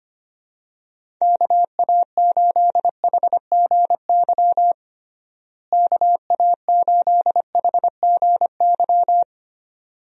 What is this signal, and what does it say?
Morse code, heard as a keyed tone, about 25 wpm, 700 Hz. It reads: KA8HGY KA8HGY
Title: USA Call Signs - 6 Characters - Repeated 2x - 25wpm